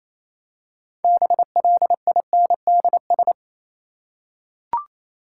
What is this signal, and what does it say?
Morse code, heard as a keyed tone, about 28 wpm, 700 Hz. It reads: BLINDS
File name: Easy Words - Speed-Racing with Hints - 22wpm